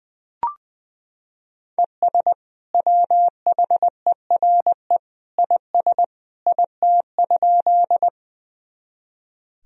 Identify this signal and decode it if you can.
Morse code, heard as a keyed tone, about 20 wpm, 700 Hz. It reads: ES WHERE IS IT?
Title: Sentences from Top 200 Words and Abbreviations 20wpm